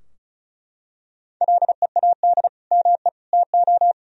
Morse code, heard as a keyed tone, about 36 wpm, 700 Hz. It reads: LEAD ME TO
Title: Sentences from Top 1000 Words - Review Part 2 - 35wpm